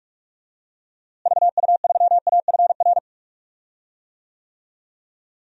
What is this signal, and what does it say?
Morse code, heard as a keyed tone, about 44 wpm, 700 Hz. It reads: VU3AFR
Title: DXCC-100 Call Signs - 6 Characters - Repeated 3x - 45wpm